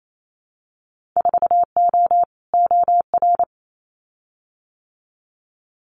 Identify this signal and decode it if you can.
Morse code, heard as a keyed tone, about 28 wpm, 700 Hz. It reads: 4O OR
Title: Winter Field Day Exchanges Encore 28wpm